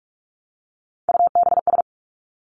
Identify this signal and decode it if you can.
Morse code, heard as a keyed tone, about 46 wpm, 700 Hz. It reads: UBS
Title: Three Challenging Letter-Number Combinations 45wpm